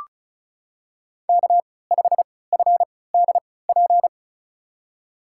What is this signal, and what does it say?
Morse code, heard as a keyed tone, about 36 wpm, spaced as for 20 wpm, 700 Hz. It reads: K5FDP
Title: USA Call Signs - 5 Characters - 3x Character Spacing - 35wpm